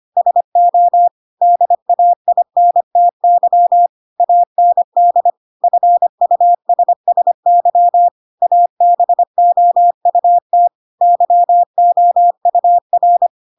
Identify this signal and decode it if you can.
Morse code, heard as a keyed tone, about 25 wpm, 700 Hz. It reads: SO DAINTY AND FUSSY ABOUT YOUR
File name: Pinocchio / Pinocchio ch7 25 25 ik1hge